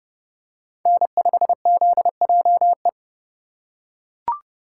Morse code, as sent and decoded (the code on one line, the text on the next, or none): -. ..... --.. .--- .
N5ZJE